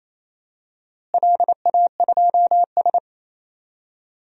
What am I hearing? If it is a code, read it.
Morse code, heard as a keyed tone, about 28 wpm, 700 Hz. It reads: LA2S
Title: DXCC-100 Call Signs - 4 Characters - Repeated 2x - 28wpm